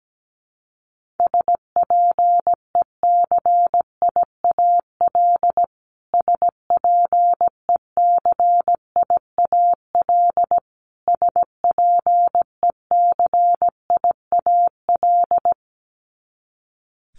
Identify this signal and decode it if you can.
Morse code, heard as a keyed tone, about 17 wpm, 700 Hz. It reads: SPECIAL SPECIAL SPECIAL
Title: Top 500 Words Repeated 17wpm